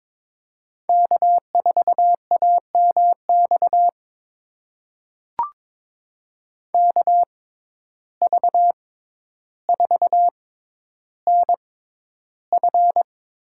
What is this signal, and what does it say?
Morse code, heard as a keyed tone, about 21 wpm, spaced as for 11 wpm, 700 Hz. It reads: K4AMX KV4NF
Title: USA Call Signs - 6x Character Spacing - 22wpm